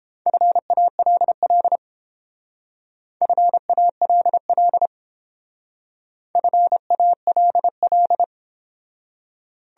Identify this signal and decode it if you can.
Morse code, heard as a keyed tone, about 29 wpm, 700 Hz. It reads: FALL FALL FALL